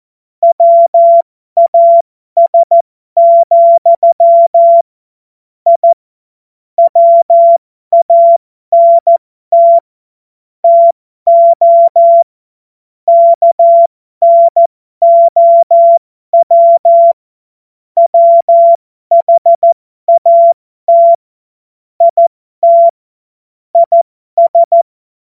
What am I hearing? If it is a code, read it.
Morse code, heard as a keyed tone, about 14 wpm, spaced as for 12 wpm, 680 Hz. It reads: WAS, I WANT TO KNOW WHAT IT IS
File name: Morsekurs / Witze